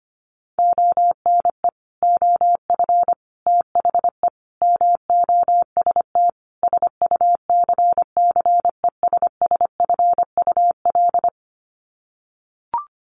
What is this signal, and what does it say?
Morse code, heard as a keyed tone, about 25 wpm, 700 Hz. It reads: ONE OF THE MOST SUCCESSFUL
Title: Sets of 5 Words Encore2 25wpm